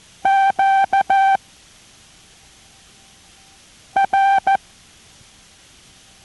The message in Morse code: Q R